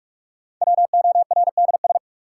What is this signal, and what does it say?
Morse code, heard as a keyed tone, about 46 wpm, 700 Hz. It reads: WORDS
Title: Sets of 3 Words - New Word Drill - 45wpm